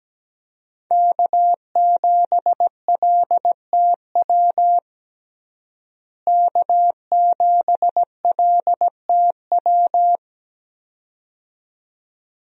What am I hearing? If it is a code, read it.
Morse code, heard as a keyed tone, about 17 wpm, 700 Hz. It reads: K7LTW K7LTW
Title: USA Call Signs - 5 Characters - Repeated 2x - 17wpm